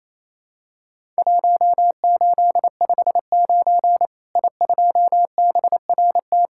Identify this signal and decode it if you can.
Morse code, heard as a keyed tone, about 28 wpm, 700 Hz. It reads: 1859 I2BRT
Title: Worked All Europe DX Contest - QTC Exchange - 28wpm